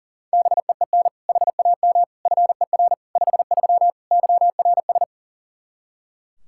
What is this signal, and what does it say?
Morse code, heard as a keyed tone, about 40 wpm, 700 Hz. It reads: BEEN HAM FER 53 YRS
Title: Practice QSO - Exchange 3 - More Info - 40wpm